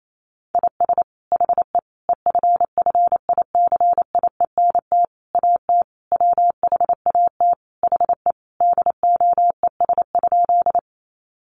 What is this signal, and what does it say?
Morse code, heard as a keyed tone, about 28 wpm, 700 Hz. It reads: IS HE EFFICIENT AT WHAT HE DOES?
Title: Sentences from Top 1300 Words 28wpm